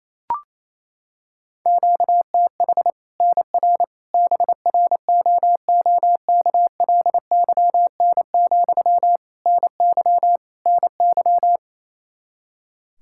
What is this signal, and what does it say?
Morse code, heard as a keyed tone, about 28 wpm, 700 Hz. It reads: QTH NR BROOKLYN, NY NY
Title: Practice QSO - Exchange 1 - Beginning - 28wpm